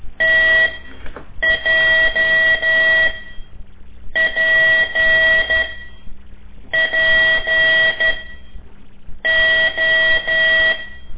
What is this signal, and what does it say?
Morse code, heard as a keyed tone, about 12 wpm, spaced as for 6 wpm, 1950 Hz. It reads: TJPPO
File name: ik1weg2 / CORSI CW / lez5-1